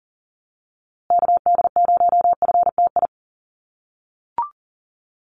Morse code, heard as a keyed tone, about 37 wpm, 700 Hz. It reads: KD0FTI
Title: USA Call Signs Repeated 3x 40wpm